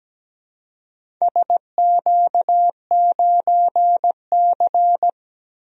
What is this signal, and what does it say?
Morse code, heard as a keyed tone, about 17 wpm, 700 Hz. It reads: SQ9C
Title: DXCC-100 Call Signs - 4 Characters - 17wpm